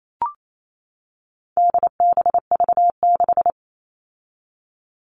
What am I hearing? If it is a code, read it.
Morse code, heard as a keyed tone, about 28 wpm, 700 Hz. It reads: DBV6
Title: All Cut Numbers - 4-Digit 28wpm